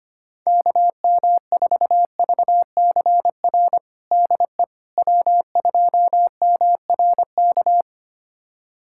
Morse code, heard as a keyed tone, about 25 wpm, 700 Hz. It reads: KM4VCR DE W2MRK
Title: USA Callsign DE Callsign 25wpm